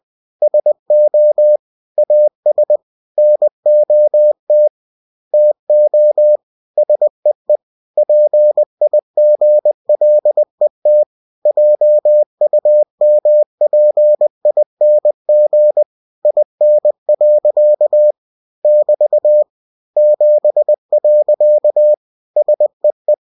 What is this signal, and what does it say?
Morse code, heard as a keyed tone, about 20 wpm, 600 Hz. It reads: SO AS NOT TO SEE PIGLET JUMPING IN. <BT> 7. SEE
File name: cw-books / winnie-the-pooh / winnie-20wpm-page-0064